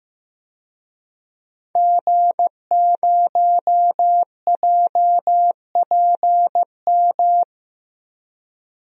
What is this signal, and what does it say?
Morse code, heard as a keyed tone, about 15 wpm, 700 Hz. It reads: G0JPM
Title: DXCC-100 Call Signs - 5 Characters - 15wpm